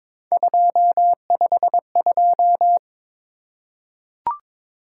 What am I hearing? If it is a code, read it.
Morse code, heard as a keyed tone, about 22 wpm, 700 Hz. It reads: 252
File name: practice / 22wpm / Three Letter-Number Combinations - 22wpm